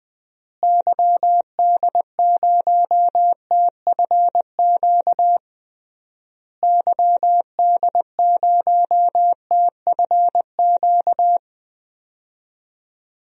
Morse code, heard as a keyed tone, about 20 wpm, 700 Hz. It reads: YD0TFQ YD0TFQ
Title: DXCC-100 Call Signs - 6 Characters - Repeated 3x - 20wpm